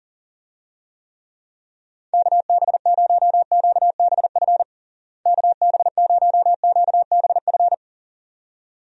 Morse code, heard as a keyed tone, about 37 wpm, 700 Hz. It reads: KB0QBF KB0QBF
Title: USA Call Signs - 6 Characters - Repeated 2x - 40wpm